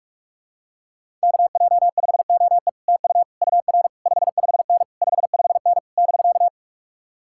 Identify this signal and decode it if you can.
Morse code, heard as a keyed tone, about 49 wpm, 700 Hz. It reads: KJ5OE TU UR 55N 55N <BK>